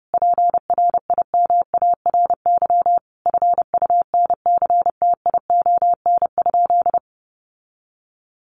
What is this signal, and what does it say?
Morse code, heard as a keyed tone, about 28 wpm, 700 Hz. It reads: PRIMARY FUNCTION?
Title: Sentences from Top 1100 Words 30wpm